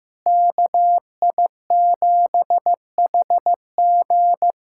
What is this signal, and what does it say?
Morse code, heard as a keyed tone, about 15 wpm, 700 Hz. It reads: KI7HG
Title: USA Callsign DE Callsign - 5 Characters - 15wpm